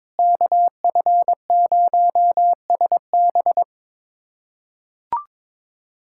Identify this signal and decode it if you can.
Morse code, heard as a keyed tone, about 22 wpm, 700 Hz. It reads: KF0SB
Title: USA Callsign DE Callsign - 5 Characters - 22wpm